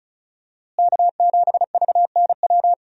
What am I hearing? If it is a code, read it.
Morse code, heard as a keyed tone, about 34 wpm, 700 Hz. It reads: K7VNW
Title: USA Call Signs - 5 Characters - Repeated 2x - 35wpm